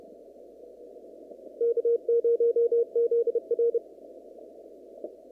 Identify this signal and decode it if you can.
Morse code, heard as a keyed tone, about 34 wpm, 470 Hz. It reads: K0ZR
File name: CW / kt5j wpx cw-op-k5pi